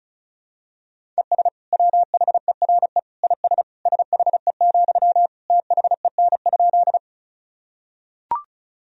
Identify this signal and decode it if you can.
Morse code, heard as a keyed tone, about 34 wpm, 700 Hz. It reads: ES WHERE IS SHE, THEN?
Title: Sentences from Top 200 Words and Abbreviations 35wpm